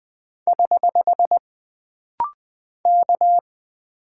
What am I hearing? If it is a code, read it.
Morse code, heard as a keyed tone, about 20 wpm, 700 Hz. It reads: <HH> K
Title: Single Character (Letter:Number:Punctuation:Prosign) - Rapid-Fire - 20wpm